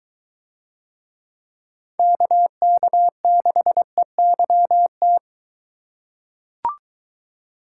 Morse code, heard as a keyed tone, about 23 wpm, 700 Hz. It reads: KK6EYT